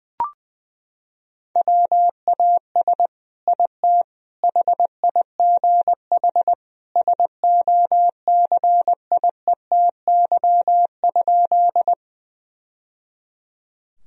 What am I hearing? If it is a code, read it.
Morse code, heard as a keyed tone, about 20 wpm, 700 Hz. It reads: WAS IT HIGH SOCIETY?
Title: Sentences from Top 1000 Words - Review Part 4 - 20wpm